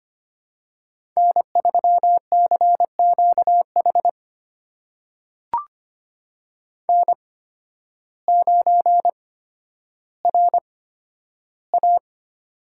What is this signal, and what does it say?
Morse code, heard as a keyed tone, about 25 wpm, spaced as for 12 wpm, 700 Hz. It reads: N3CQH N9RA
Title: USA Call Signs - 8x Character Spacing - 25wpm